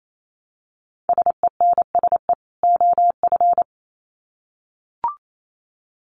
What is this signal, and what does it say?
Morse code, heard as a keyed tone, about 28 wpm, 700 Hz. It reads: SENSE OF
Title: Sets of 2 Words 28wpm